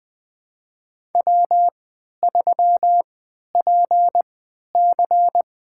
Morse code, heard as a keyed tone, about 20 wpm, spaced as for 11 wpm, 700 Hz. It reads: W3PC
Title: USA Call Signs - 4 Characters - 3x Character Spacing - 20wpm